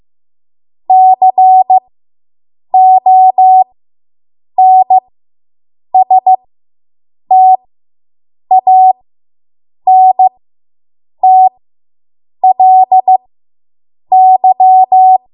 Morse code, heard as a keyed tone, about 15 wpm, spaced as for 7 wpm, 750 Hz. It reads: CONSTANTLY